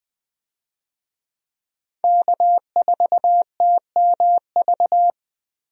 Morse code, heard as a keyed tone, about 20 wpm, 700 Hz. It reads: K4TMV
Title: USA Call Signs - 5 Characters - 20wpm